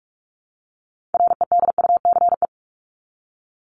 Morse code, heard as a keyed tone, about 44 wpm, 700 Hz. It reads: REDUCE
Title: Top 1001-1250 Words 45wpm